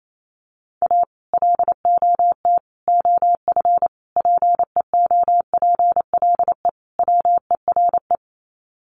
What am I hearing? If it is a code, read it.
Morse code, heard as a keyed tone, about 28 wpm, 700 Hz. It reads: A LOT OF PEOPLE WERE
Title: Sets of 5 Words Encore2 28wpm